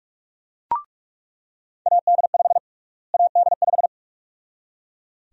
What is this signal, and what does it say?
Morse code, heard as a keyed tone, about 41 wpm, 700 Hz. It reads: AD5 AD5
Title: USA Call Sign Prefixes - Repeated 2x - 45wpm